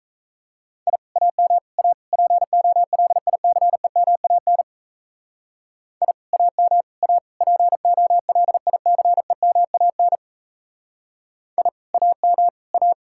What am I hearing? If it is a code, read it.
Morse code, heard as a keyed tone, about 34 wpm, 700 Hz. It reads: I AM A POLICEMAN I AM A POLICEMAN I AM A
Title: Speed-Racing Common QSO Phrases Encore - 28wpm